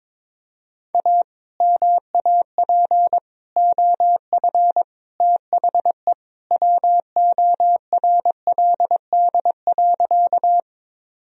Morse code, heard as a keyed tone, about 22 wpm, 700 Hz. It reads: A MAP OF THE WORLD.
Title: Sentences from Top 1000 Words - Review Part 2 - 22wpm